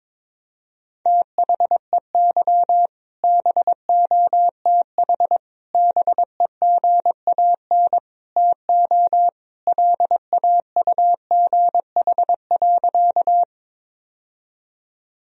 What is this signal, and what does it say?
Morse code, heard as a keyed tone, about 22 wpm, 700 Hz. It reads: THEY BOTH BEGAN TO LAUGH.
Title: Sentences from Top 1000 Words - Review Part 1 - 22wpm